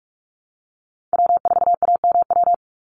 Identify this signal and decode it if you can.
Morse code, heard as a keyed tone, about 46 wpm, 690 Hz. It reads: W4AMW
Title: NA SOTA Chaser Callsigns 45wpm